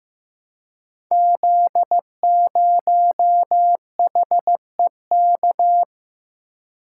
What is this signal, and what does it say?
Morse code, heard as a keyed tone, about 15 wpm, 700 Hz. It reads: Z0HEK
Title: Code Groups - 5 Letter:Numbers - 15wpm